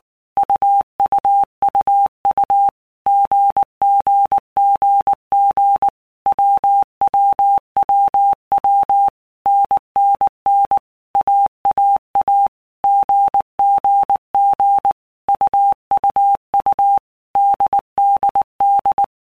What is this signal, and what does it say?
Morse code, heard as a keyed tone, about 19 wpm, 800 Hz. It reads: UUUU GGGG WWWW NNN AAA GGG UUU DDD
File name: Basic Letter Groups By Blocks / Groups@18 / AWUNDG@18